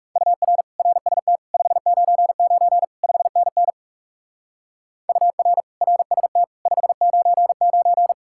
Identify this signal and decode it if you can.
Morse code, heard as a keyed tone, about 38 wpm, 700 Hz. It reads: UR RST 599 5NN UR RST 599